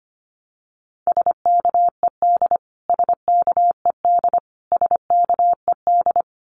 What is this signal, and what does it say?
Morse code, heard as a keyed tone, about 25 wpm, 700 Hz. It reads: SKED SKED SKED
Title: Abbreviations Repeated 3x - 25wpm